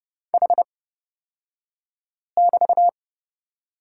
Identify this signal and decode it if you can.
Morse code, heard as a keyed tone, about 30 wpm, spaced as for 5 wpm, 700 Hz. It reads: H<BT>
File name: Lesson 38d - Letter q - Two Characters 30wpm